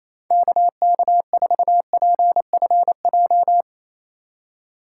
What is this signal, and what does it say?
Morse code, heard as a keyed tone, about 33 wpm, 700 Hz. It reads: KK4PFJ
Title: Speed Racing - USA 2x3 Call Signs 28wpm